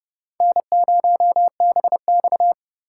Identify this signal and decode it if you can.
Morse code, heard as a keyed tone, about 40 wpm, 700 Hz. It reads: N0BX